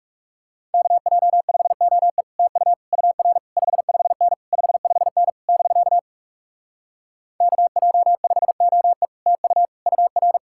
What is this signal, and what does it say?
Morse code, heard as a keyed tone, about 46 wpm, 700 Hz. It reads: KJ5OE TU UR 55N 55N <BK> KJ5OE TU UR